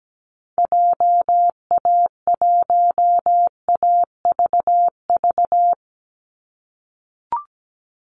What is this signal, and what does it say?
Morse code, heard as a keyed tone, about 17 wpm, 700 Hz. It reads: JA1AVV
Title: USA Call Signs 17wpm